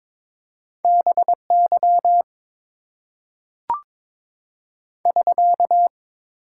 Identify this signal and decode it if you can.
Morse code, heard as a keyed tone, about 22 wpm, 700 Hz. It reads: BY <SK>
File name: Top 90 QSO Elements - 8x Character Spacing - 22wpm